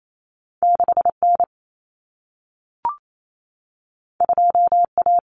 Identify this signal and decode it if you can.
Morse code, heard as a keyed tone, about 28 wpm, 700 Hz. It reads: 6N 2A
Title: Common Cut Numbers - 2-Digit 28wpm